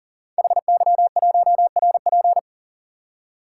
Morse code, heard as a keyed tone, about 39 wpm, 700 Hz. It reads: HY1RP
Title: Code Groups - 5 Letter:Numbers - 40wpm